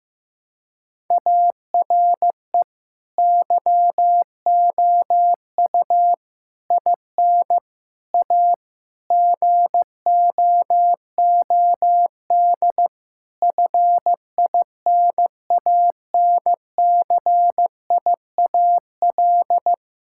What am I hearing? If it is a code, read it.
Morse code, heard as a keyed tone, about 16 wpm, 700 Hz. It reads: ARE YOU IN A GOOD FINANCIAL
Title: Sentences from Top 1000 Words - Review Part 5 - 15wpm